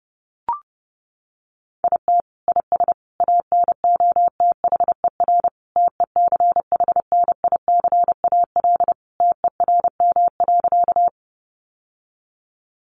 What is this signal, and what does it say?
Morse code, heard as a keyed tone, about 28 wpm, 700 Hz. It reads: IT IS ANOTHER TECHNICAL TERM.
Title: Sentences from Top 1100 Words 30wpm